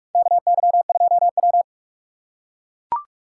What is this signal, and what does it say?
Morse code, heard as a keyed tone, about 41 wpm, 700 Hz. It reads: KY2W
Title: USA Call Signs - 4 Characters - 4x Character Spacing - 45wpm